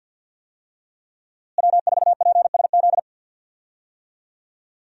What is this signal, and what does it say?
Morse code, heard as a keyed tone, about 46 wpm, 700 Hz. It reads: W4PSZ
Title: USA Call Signs - 5 Characters - Repeated 3x - 50wpm